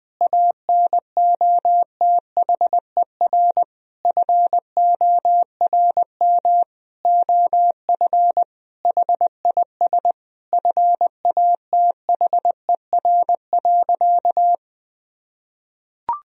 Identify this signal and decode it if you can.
Morse code, heard as a keyed tone, about 20 wpm, 700 Hz. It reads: ANOTHER FORM OF HIS FATHER.
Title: Sentences from Top 1000 Words - Review Part 1 - 20wpm